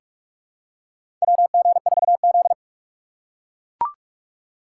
Morse code, heard as a keyed tone, about 44 wpm, 700 Hz. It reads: WG4Z
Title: USA Call Signs - 4x Character Spacing - 45wpm